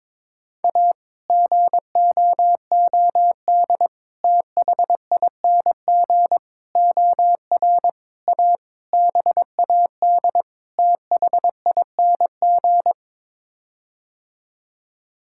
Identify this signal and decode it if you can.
Morse code, heard as a keyed tone, about 22 wpm, 700 Hz. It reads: A GOOD THING OR A BAD THING
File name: practice / 22wpm / Sets of 7 Words - 22wpm